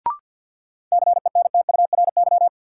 Morse code, heard as a keyed tone, about 49 wpm, 700 Hz. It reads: CENTURY
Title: Sets of 5 Words - New Word Drill - 50wpm